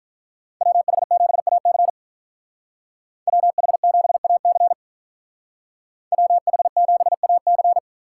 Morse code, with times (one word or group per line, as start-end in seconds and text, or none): 0.6-1.9 WH7AC
3.3-4.7 WH7AC
6.1-7.8 WH7AC